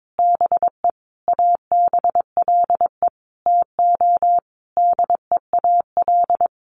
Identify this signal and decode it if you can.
Morse code, heard as a keyed tone, about 22 wpm, 700 Hz. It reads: BE ABLE TO DEAL
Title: Sentences from Top 800 Words 22wpm